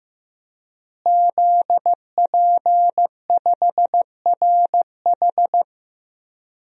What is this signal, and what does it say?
Morse code, heard as a keyed tone, about 15 wpm, 700 Hz. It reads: ZP5RH
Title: DXCC-100 Call Signs - 5 Characters - Repeated 3x - 15wpm